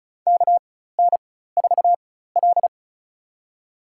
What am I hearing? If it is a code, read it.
Morse code, heard as a keyed tone, about 34 wpm, spaced as for 16 wpm, 700 Hz. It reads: KN4L
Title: USA Call Signs - 4x Character Spacing - 35wpm